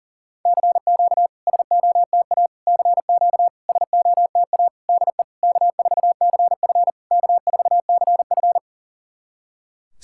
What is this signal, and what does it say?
Morse code, heard as a keyed tone, about 40 wpm, 700 Hz. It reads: CQ SOTA CQ SOTA DE K4CF K4CF
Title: SOTA QSO Practice 40wpm